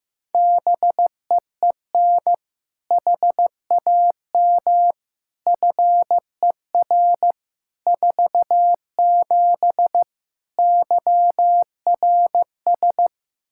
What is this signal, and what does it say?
Morse code, heard as a keyed tone, about 15 wpm, 700 Hz. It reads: BEEN HAM FER 47 YRS
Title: Practice QSO - Exchange 3 - More Info - 15wpm